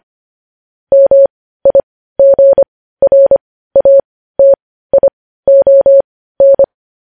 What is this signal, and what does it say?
Morse code, heard as a keyed tone, about 25 wpm, spaced as for 15 wpm, 550 Hz. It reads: MIGRATION
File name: podcast / News headlines-1x-25x15